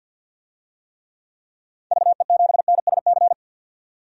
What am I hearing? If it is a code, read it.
Morse code, heard as a keyed tone, about 52 wpm, 700 Hz. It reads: VE7NSC
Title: DXCC-100 Call Signs - 6 Characters - 50wpm